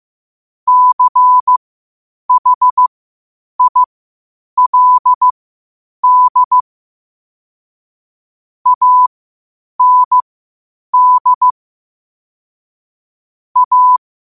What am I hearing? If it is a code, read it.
Morse code, heard as a keyed tone, about 15 wpm, spaced as for 8 wpm, 1000 Hz. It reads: CHILD AND A
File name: qotdfiles / 8wpm